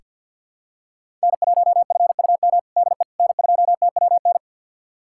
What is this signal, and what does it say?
Morse code, heard as a keyed tone, about 48 wpm, 700 Hz. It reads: N1FUM DE N2TWN
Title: USA Callsign DE Callsign - 5 Characters - 50wpm